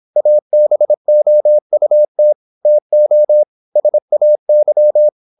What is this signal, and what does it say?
Morse code, heard as a keyed tone, about 26 wpm, 600 Hz. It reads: ABOUT TO SAY
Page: Never 26wpm | CW med Gnister